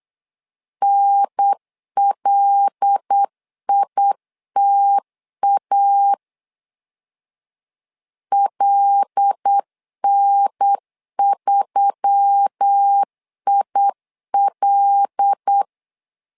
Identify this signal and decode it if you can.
Morse code, heard as a keyed tone, about 8 wpm, 800 Hz. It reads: NLITA LN3IL
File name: LEZIONI / 5 LBVC / 5 L 8wpm 2831